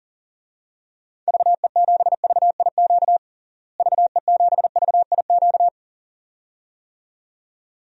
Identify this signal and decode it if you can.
Morse code, heard as a keyed tone, about 40 wpm, 700 Hz. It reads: VE7VIQ VE7VIQ
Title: DXCC-100 Call Signs - 6 Characters - Repeated 2x - 40wpm